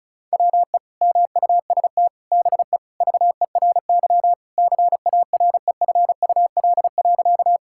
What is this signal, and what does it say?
Morse code, heard as a keyed tone, about 36 wpm, 700 Hz. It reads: WE MUST BE VERY CAREFUL.
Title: Sentences from Top 1600 Words 35wpm